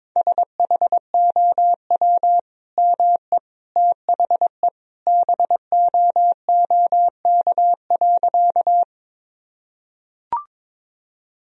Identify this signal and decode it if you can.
Morse code, heard as a keyed tone, about 22 wpm, 700 Hz. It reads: SHOW ME THE BOOK.
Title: Sentences from Top 1000 Words - Review Part 1 - 22wpm